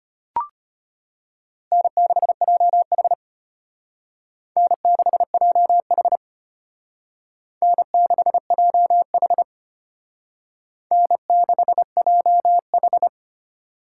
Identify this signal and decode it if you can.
Morse code, heard as a keyed tone, about 33 wpm, 700 Hz. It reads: N6JH N6JH N6JH N6JH